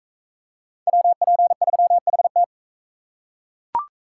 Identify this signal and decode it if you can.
Morse code, heard as a keyed tone, about 36 wpm, 700 Hz. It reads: WP3HT